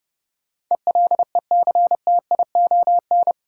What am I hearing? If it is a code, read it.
Morse code, heard as a keyed tone, about 30 wpm, 700 Hz. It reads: ELECTION
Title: Sentences from Top 1500 Words - New Word Drill Alt - 30wpm